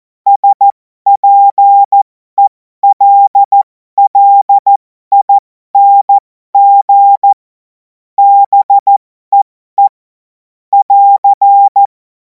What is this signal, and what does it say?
Morse code, heard as a keyed tone, about 14 wpm, spaced as for 12 wpm, 800 Hz. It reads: SPELLING BEE <AR>
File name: Morsekurs / Witze